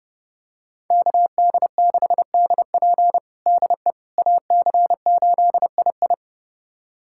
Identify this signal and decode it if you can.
Morse code, heard as a keyed tone, about 30 wpm, 700 Hz. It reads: KD6DP DE AC8II
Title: USA Callsign DE Callsign - 5 Characters - 30wpm